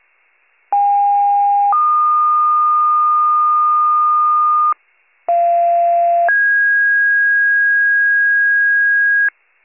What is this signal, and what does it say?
Two-tone page, heard as a one-second tone, then a three-second tone, about 810 then 1220 Hz, about 690 then 1650 Hz, with noise. It is heard from 0.7 to 9.3 s.